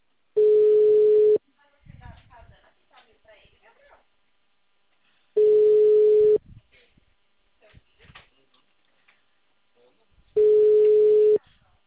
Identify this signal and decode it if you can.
Ringback tone (Europe, 425 Hz), with speech.